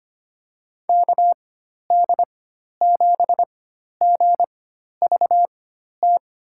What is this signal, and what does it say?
Morse code, heard as a keyed tone, about 25 wpm, spaced as for 12 wpm, 700 Hz. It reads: KD7GVT
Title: USA Call Signs - 6 Characters - 4x Character Spacing - 25wpm